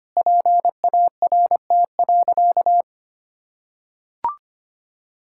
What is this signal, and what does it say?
Morse code, heard as a keyed tone, about 25 wpm, 700 Hz. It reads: PART.
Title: Sentences from Top 1000 Words - Review Part 6 - 25wpm